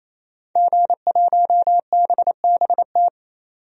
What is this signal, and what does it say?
Morse code, heard as a keyed tone, about 28 wpm, 700 Hz. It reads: G1BBT
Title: DXCC-100 Call Signs - 5 Characters - 28wpm